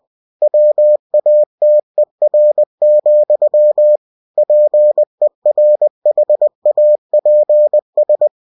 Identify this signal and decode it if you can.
Morse code, heard as a keyed tone, about 20 wpm, 600 Hz. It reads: WATER, PERHAPS
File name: cw-books / winnie-the-pooh / winnie-20wpm-page-0092